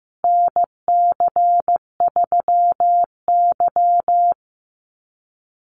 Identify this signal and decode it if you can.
Morse code, heard as a keyed tone, about 15 wpm, 700 Hz. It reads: NC3Y
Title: Call Signs Repeated 2x - 15wpm